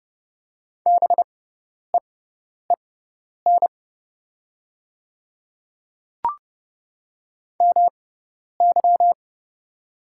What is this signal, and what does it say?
Morse code, heard as a keyed tone, about 30 wpm, spaced as for 10 wpm, 700 Hz. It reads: BEEN MY
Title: 6x Character Spacing - Top 300 Words - 30wpm